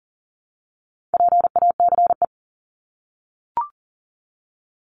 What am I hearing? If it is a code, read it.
Morse code, heard as a keyed tone, about 40 wpm, 700 Hz. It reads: PACE
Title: Top 1751-2000 Words 40wpm